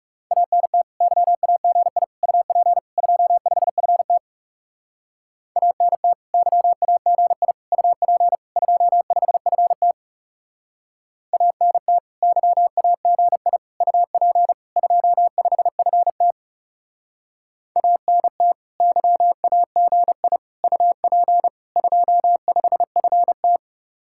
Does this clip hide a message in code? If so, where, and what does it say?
Yes: Morse code from 0.3 to 23.6 s, heard as a keyed tone, about 38 wpm, 700 Hz. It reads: ANT YAGI UP 25FT ANT YAGI UP 25FT ANT YAGI UP 25FT ANT YAGI UP 25FT